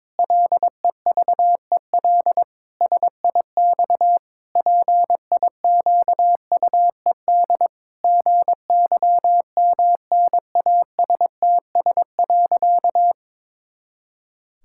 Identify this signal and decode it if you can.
Morse code, heard as a keyed tone, about 22 wpm, 700 Hz. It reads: LEVEL SIX PIQUED GYMNASTS.
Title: Pangram Challenge v2 - 22wpm